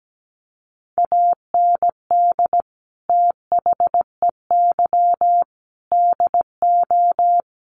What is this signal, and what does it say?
Morse code, heard as a keyed tone, about 17 wpm, 700 Hz. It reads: AND THEY DO
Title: Sets of 5 Words Encore2 17wpm